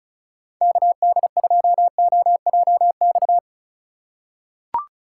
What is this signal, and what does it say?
Morse code, heard as a keyed tone, about 34 wpm, 700 Hz. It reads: KD2OJX
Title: USA Call Signs - 6 Characters - 2x Character Spacing - 35wpm